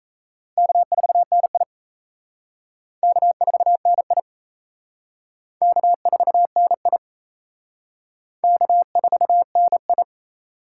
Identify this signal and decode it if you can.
Morse code, heard as a keyed tone, about 36 wpm, 700 Hz. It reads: K4NI K4NI K4NI K4NI